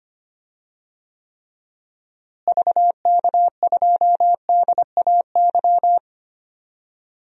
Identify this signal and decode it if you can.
Morse code, heard as a keyed tone, about 25 wpm, 700 Hz. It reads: VK2DAY
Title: DXCC-100 Call Signs - 6 Characters - Repeated 3x - 25wpm